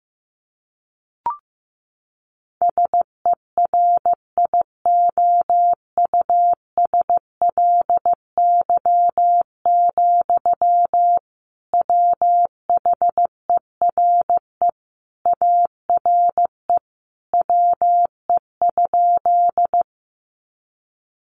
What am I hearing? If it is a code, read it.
Morse code, heard as a keyed tone, about 15 wpm, 700 Hz. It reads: SERIOUSLY, WHERE ARE WE?
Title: Sentences from Top 1200 Words 15wpm